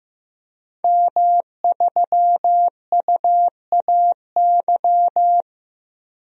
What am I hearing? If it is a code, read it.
Morse code, heard as a keyed tone, about 15 wpm, 700 Hz. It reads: M3UAY
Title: DXCC-100 Call Signs - 5 Characters - 15wpm